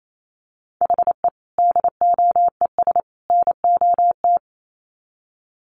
Morse code, heard as a keyed tone, about 28 wpm, 700 Hz. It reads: HE DOES NOT
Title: Sets of 3 Words 28wpm